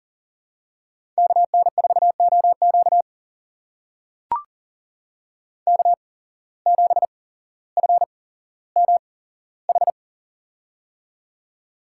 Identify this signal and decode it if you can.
Morse code, heard as a keyed tone, about 40 wpm, spaced as for 17 wpm, 700 Hz. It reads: KN4OQ K7FMH
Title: USA Call Signs - 5 Characters - 8x Character Spacing - 40wpm